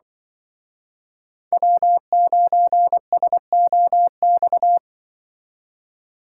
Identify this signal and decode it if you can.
Morse code, heard as a keyed tone, about 23 wpm, 700 Hz. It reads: W9SOX
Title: Speed Racing - USA 1x3 Call Signs 20wpm